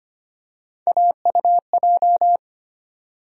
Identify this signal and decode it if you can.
Morse code, heard as a keyed tone, about 25 wpm, 700 Hz. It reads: AUJ
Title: USA Call Sign Postfixes - 3 Letters 25wpm